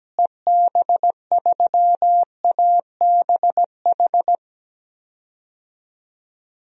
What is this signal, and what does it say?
Morse code, heard as a keyed tone, about 17 wpm, 700 Hz. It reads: EB3ABH
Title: DXCC-100 Call Signs - 6 Characters - Repeated 3x - 17wpm